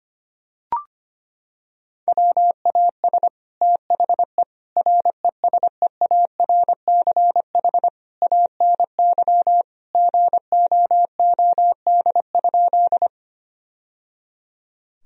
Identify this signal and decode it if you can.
Morse code, heard as a keyed tone, about 25 wpm, 700 Hz. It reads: WAS THE RESEARCH ANY GOOD?
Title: Sentences from Top 1000 Words - Review Part 4 - 25wpm